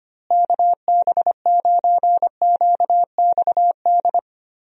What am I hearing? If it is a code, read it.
Morse code, heard as a keyed tone, about 25 wpm, 700 Hz. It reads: KB9QXD
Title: USA Call Signs - 6 Characters - Repeated 3x - 25wpm